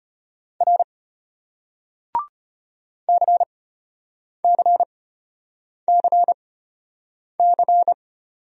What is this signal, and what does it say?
Morse code, heard as a keyed tone, about 34 wpm, spaced as for 8 wpm, 700 Hz. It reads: R CCCC